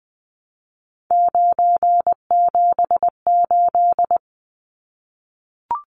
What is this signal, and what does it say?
Morse code, heard as a keyed tone, about 20 wpm, 700 Hz. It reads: 978
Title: Numbers - 3-Digit 20wpm